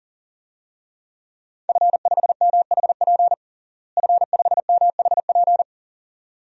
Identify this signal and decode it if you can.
Morse code, heard as a keyed tone, about 40 wpm, 700 Hz. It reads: F5MHP F5MHP
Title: DXCC-100 Call Signs - 5 Characters - Repeated 2x - 40wpm